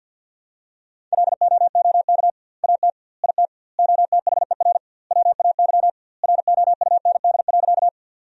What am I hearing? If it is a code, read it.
Morse code, heard as a keyed tone, about 51 wpm, 700 Hz. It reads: LOOK AT IT OTHER WAY ROUND.
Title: 2x Word-Spacing - Sentences from Top 200 Words - 50wpm